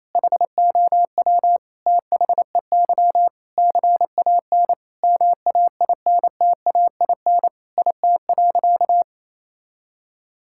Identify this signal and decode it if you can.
Morse code, heard as a keyed tone, about 28 wpm, 700 Hz. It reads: HOW THEY CAN MAINTAIN IT.
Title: Sentences from Top 1000 Words - Review Part 5 - 28wpm